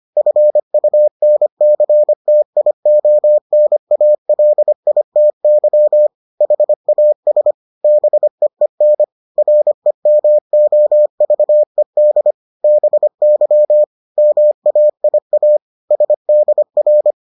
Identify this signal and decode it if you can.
Morse code, heard as a keyed tone, about 25 wpm, 600 Hz. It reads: FUNCTIONALITY HAS BEEN REMOVED BY MAIA SDR